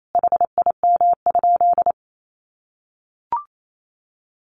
Morse code, heard as a keyed tone, about 28 wpm, 700 Hz. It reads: HIM?
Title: Sentences from Top 700 Words 28wpm